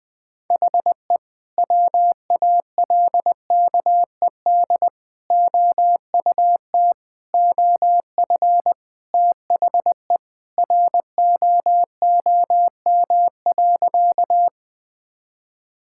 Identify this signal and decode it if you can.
Morse code, heard as a keyed tone, about 20 wpm, 700 Hz. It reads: HE WALKED OUT OF THE ROOM.
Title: Sentences with 2-Word Phrases - 20wpm